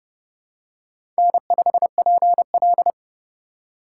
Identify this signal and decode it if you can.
Morse code, heard as a keyed tone, about 30 wpm, 700 Hz. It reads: N5PL
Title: USA Call Signs - 4 Characters - 4x Character Spacing - 30wpm